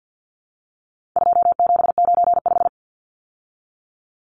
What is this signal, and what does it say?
Morse code, heard as a keyed tone, about 49 wpm, 700 Hz. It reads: 2795
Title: Numbers - 4-Digit 50wpm